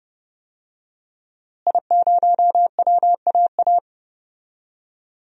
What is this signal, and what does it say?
Morse code, heard as a keyed tone, about 29 wpm, 700 Hz. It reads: I0WAA
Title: DXCC-100 Call Signs - 5 Characters - Repeated 2x - 30wpm